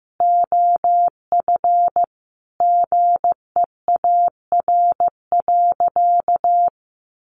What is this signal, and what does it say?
Morse code, heard as a keyed tone, about 15 wpm, 700 Hz. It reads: OF GEAR.
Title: Sentences from Top 1400 Words 15wpm